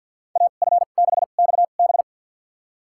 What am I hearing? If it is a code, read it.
Morse code, heard as a keyed tone, about 51 wpm, spaced as for 36 wpm, 700 Hz. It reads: AF6XB
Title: USA Call Signs - 5 Characters - 2x Character Spacing - 50wpm